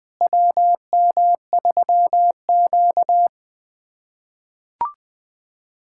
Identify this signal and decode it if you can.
Morse code, heard as a keyed tone, about 21 wpm, 700 Hz. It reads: WM3Q
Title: USA Callsign DE Callsign - 4 Characters - 20wpm